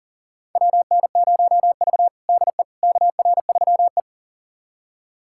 Morse code, heard as a keyed tone, about 40 wpm, 700 Hz. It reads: WN0V DE KR3E
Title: USA Callsign DE Callsign - 4 Characters - 40wpm